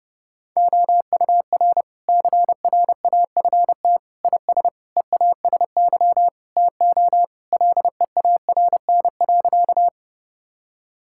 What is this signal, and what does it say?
Morse code, heard as a keyed tone, about 32 wpm, 700 Hz. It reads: OUR CRAFT IS EASY TO LEARN.
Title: Sentences from Top 1000 Words - Review Part 6 - 30wpm